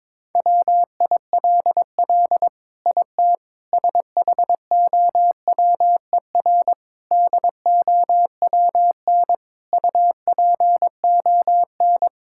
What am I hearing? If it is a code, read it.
Morse code, heard as a keyed tone, about 22 wpm, 700 Hz. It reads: WILL IT SHOWER DOWN UPON
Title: Sentences from Top 1600 Words 22wpm